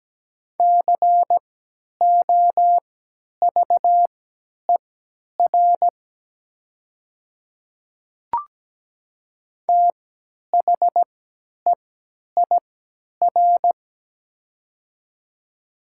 Morse code, heard as a keyed tone, about 17 wpm, spaced as for 10 wpm, 700 Hz. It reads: COVER THEIR
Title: 3x Character Spacing - Top 300 Words - 17wpm